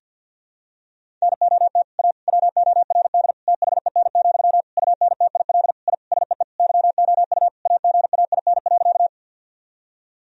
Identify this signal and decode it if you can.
Morse code, heard as a keyed tone, about 48 wpm, 700 Hz. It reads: NOT A WORD THEN, UNTIL I SEE YOU AGAIN.
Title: Sentences from Top 1000 Words - Review Part 1 - 50wpm